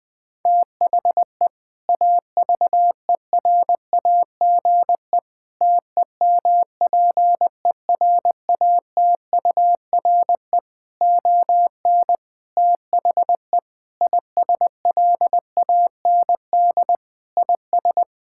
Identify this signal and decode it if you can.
Morse code, heard as a keyed tone, about 20 wpm, 700 Hz. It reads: THE AVERAGE TEMPERATURE ON THE ISLAND IS
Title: Sentences from Top 1000 Words - Review Part 6 - 20wpm